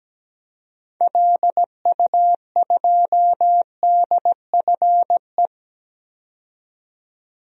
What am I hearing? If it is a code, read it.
Morse code, heard as a keyed tone, about 17 wpm, 700 Hz. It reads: LU2DFE
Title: DXCC-100 Call Signs - 6 Characters - 17wpm